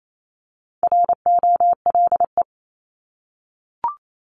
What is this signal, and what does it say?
Morse code, heard as a keyed tone, about 28 wpm, 700 Hz. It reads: ROLE
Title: Top 751-1000 Words 28wpm